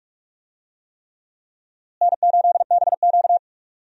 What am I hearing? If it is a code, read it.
Morse code, heard as a keyed tone, about 46 wpm, 700 Hz. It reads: N8BQ
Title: USA Call Signs - 4 Characters - Repeated 3x - 45wpm